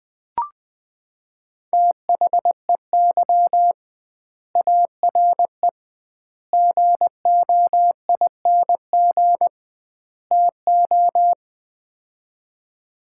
Morse code, heard as a keyed tone, about 20 wpm, 700 Hz. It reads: THEY ARE GOING TO
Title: 2x Word-Spacing - Sets of 4 Words - 20wpm